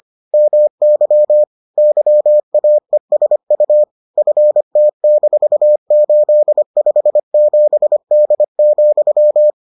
Morse code, heard as a keyed tone, about 25 wpm, 600 Hz. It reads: MY YAESU FT-857D,